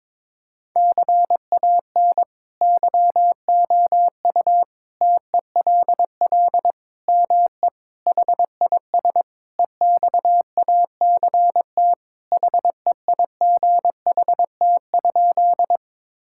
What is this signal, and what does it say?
Morse code, heard as a keyed tone, about 22 wpm, 700 Hz. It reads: CAN YOU TELL ME HIS EXACT HEIGHT?
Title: Sentences from Top 1600 Words 22wpm